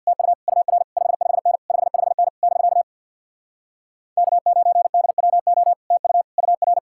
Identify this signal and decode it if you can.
Morse code, heard as a keyed tone, about 61 wpm, 700 Hz. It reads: TU UR 55N 55N <BK> K9DWO TU UR
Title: Speed-Racing Common QSO Phrases - 50wpm